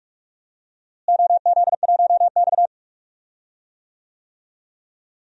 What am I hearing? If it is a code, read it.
Morse code, heard as a keyed tone, about 46 wpm, 700 Hz. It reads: OZ1X
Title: DXCC-100 Call Signs - 4 Characters - Repeated 3x - 45wpm